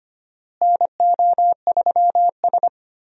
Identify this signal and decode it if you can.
Morse code, heard as a keyed tone, about 25 wpm, 700 Hz. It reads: NO3S
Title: USA Call Signs - 4 Characters - Repeated 3x - 25wpm